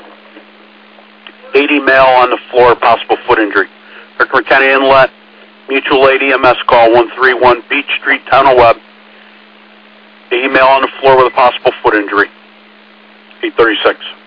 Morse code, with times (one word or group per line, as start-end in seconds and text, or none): none